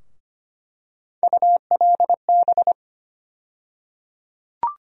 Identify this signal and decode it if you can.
Morse code, heard as a keyed tone, about 25 wpm, 700 Hz. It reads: ULB
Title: USA Call Sign Postfixes - 3 Letters 25wpm